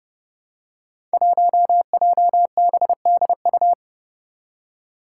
Morse code, heard as a keyed tone, about 30 wpm, 700 Hz. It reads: 1JBDU
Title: Five Challenging Letter-Number Combinations 30wpm